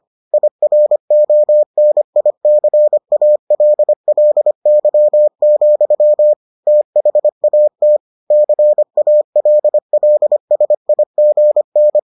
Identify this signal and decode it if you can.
Morse code, heard as a keyed tone, about 25 wpm, 600 Hz. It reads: IRONICALLY, THAT CALLSIGN